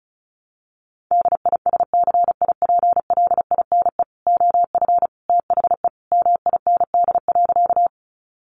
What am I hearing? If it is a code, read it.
Morse code, heard as a keyed tone, about 34 wpm, 700 Hz. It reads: DISCIPLINE OF THE MIND.
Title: Sentences from Top 900 Words 35wpm